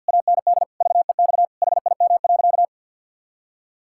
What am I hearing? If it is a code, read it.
Morse code, heard as a keyed tone, about 51 wpm, 700 Hz. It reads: AND VEX HIM.
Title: Pangram Challenge v2 - 50wpm